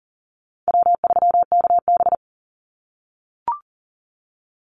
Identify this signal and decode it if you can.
Morse code, heard as a keyed tone, about 40 wpm, 700 Hz. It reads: W3KB
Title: Easy Call Signs - 40wpm